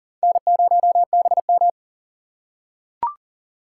Morse code, heard as a keyed tone, about 44 wpm, 700 Hz. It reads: N0BM
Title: USA Call Signs - 4 Characters - 4x Character Spacing - 40wpm